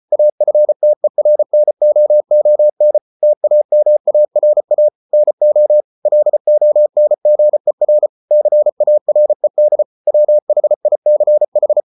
Morse code, heard as a keyed tone, about 34 wpm, 600 Hz. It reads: AFTERNOON TAMARA NO LONGER CARED WHICH